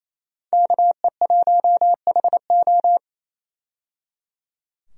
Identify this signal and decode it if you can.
Morse code, heard as a keyed tone, about 28 wpm, 700 Hz. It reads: KE1HO
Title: USA Call Signs - 5 Characters - 28wpm